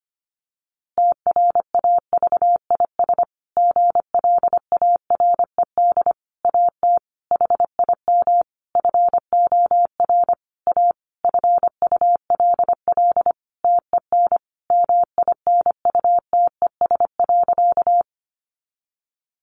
Morse code, heard as a keyed tone, about 25 wpm, 700 Hz. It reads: TRAVIS GLARED AT HIM FOR A FULL TEN MINUTES.
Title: A Sound Of Thunder 25wpm